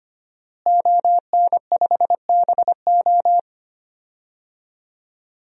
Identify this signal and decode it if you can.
Morse code, heard as a keyed tone, about 25 wpm, 700 Hz. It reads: ON5BO
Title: DXCC-100 Call Signs - 5 Characters - Repeated 2x - 25wpm